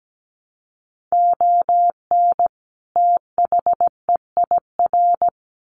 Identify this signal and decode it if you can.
Morse code, heard as a keyed tone, about 17 wpm, 700 Hz. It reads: ON THEIR
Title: Sets of 2 Words 17wpm